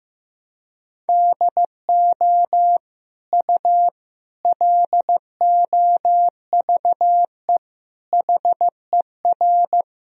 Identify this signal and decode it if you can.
Morse code, heard as a keyed tone, about 15 wpm, 700 Hz. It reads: DO U LOVE HER
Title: Sentences from Top 400 Words and Abbreviations 15wpm